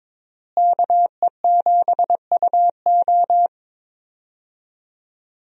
Morse code, heard as a keyed tone, about 22 wpm, 700 Hz. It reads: KE7UO
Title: USA Call Signs - 5 Characters - Repeated 3x - 22wpm